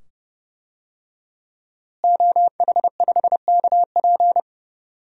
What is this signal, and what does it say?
Morse code, heard as a keyed tone, about 30 wpm, 700 Hz. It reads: OH5KP
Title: DXCC-100 Call Signs - 5 Characters - Repeated 2x - 30wpm